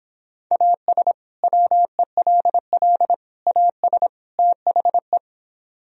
Morse code, heard as a keyed tone, about 26 wpm, 700 Hz. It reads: AS WELL AS THE